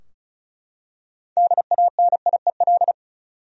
Morse code, heard as a keyed tone, about 36 wpm, 700 Hz. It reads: DANIEL
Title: USA Common Names - 35wpm